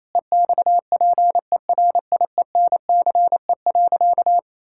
Morse code, heard as a keyed tone, about 28 wpm, 700 Hz. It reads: EXPERIENCE.
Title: Sentences from Top 1000 Words - Review Part 6 - 28wpm